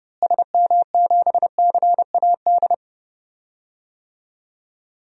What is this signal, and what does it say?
Morse code, heard as a keyed tone, about 30 wpm, 700 Hz. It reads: SM7CAD
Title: DXCC-100 Call Signs - 6 Characters - Repeated 2x - 30wpm